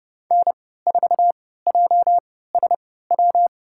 Morse code, heard as a keyed tone, about 30 wpm, spaced as for 17 wpm, 700 Hz. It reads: N4JSW
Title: USA Call Signs - 5 Characters - 3x Character Spacing - 30wpm